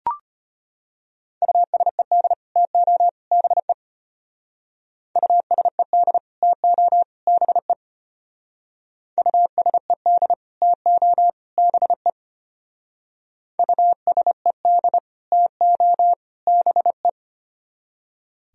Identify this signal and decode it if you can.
Morse code, heard as a keyed tone, about 33 wpm, 700 Hz. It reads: USED TO BE USED TO BE USED TO BE USED TO BE